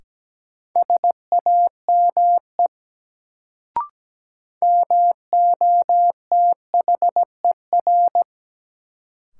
Morse code, heard as a keyed tone, about 17 wpm, 700 Hz. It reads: SAME MOTHER
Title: Sentences from Top 200 Words - New Word Drill - 17wpm